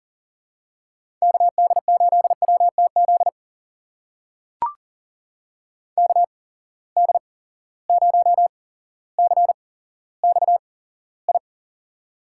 Morse code, heard as a keyed tone, about 38 wpm, spaced as for 17 wpm, 700 Hz. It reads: KD8WTZ KD0CXI
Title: USA Call Signs - 6 Characters - 8x Character Spacing - 40wpm